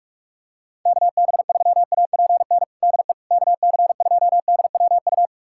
Morse code, heard as a keyed tone, about 46 wpm, 700 Hz. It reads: KB3APN DE KC2DWU
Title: USA Callsign DE Callsign - 6 Characters - 45wpm